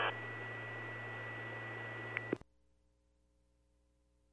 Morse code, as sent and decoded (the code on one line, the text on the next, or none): none